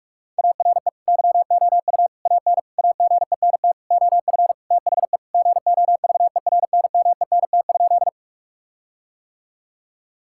Morse code, heard as a keyed tone, about 46 wpm, 700 Hz. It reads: ARE YOU AN AGENT OF THE GOVERNMENT?
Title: Sentences from Top 1600 Words 45wpm